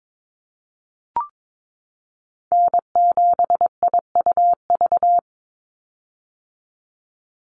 Morse code, heard as a keyed tone, about 22 wpm, 700 Hz. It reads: N7IUV
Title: USA Call Signs 22wpm